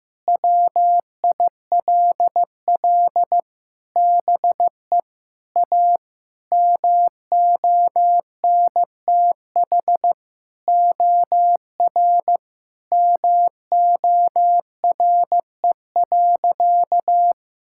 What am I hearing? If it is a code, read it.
Morse code, heard as a keyed tone, about 14 wpm, 700 Hz. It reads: WILL BE A MONTH OR MORE.
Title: Sentences from Top 1000 Words - Review Part 5 - 15wpm